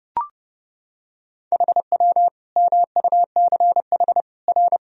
Morse code, heard as a keyed tone, about 30 wpm, 700 Hz. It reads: HW MUCH R
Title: Sentences from Top 200 Words and Abbreviations 30wpm